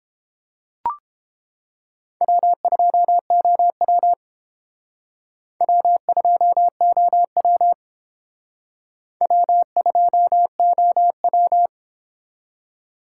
Morse code, heard as a keyed tone, about 28 wpm, 700 Hz. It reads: W2OW W2OW W2OW